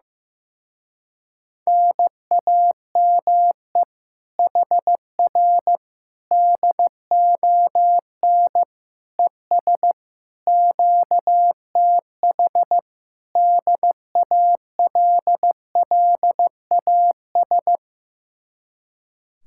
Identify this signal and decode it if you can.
Morse code, heard as a keyed tone, about 15 wpm, 700 Hz. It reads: NAME HR DON ES QTH DALLAS
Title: Speed-Racing Common QSO Phrases Encore - 15wpm